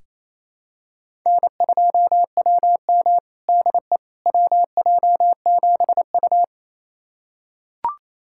Morse code, heard as a keyed tone, about 28 wpm, 700 Hz. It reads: N2WM DE WJ7U
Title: USA Callsign DE Callsign - 4 Characters - 28wpm